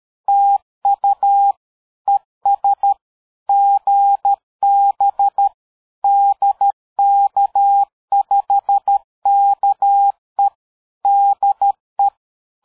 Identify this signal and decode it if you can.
Morse code, heard as a keyed tone, about 13 wpm, 800 Hz. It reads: TU ES GB DK5KE DE
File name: dk5ke / QSO 60BpM